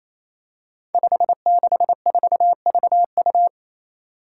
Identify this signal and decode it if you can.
Morse code, heard as a keyed tone, about 28 wpm, 700 Hz. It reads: 564VU
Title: Five Challenging Letter-Number Combinations 28wpm